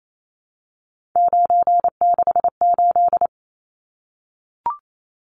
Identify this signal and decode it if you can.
Morse code, heard as a keyed tone, about 28 wpm, 700 Hz. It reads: 968
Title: Numbers - 3-Digit 28wpm